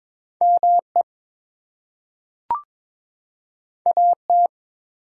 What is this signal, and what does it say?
Morse code, heard as a keyed tone, about 22 wpm, 700 Hz. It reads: ME AT
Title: Two Letter Words - 22wpm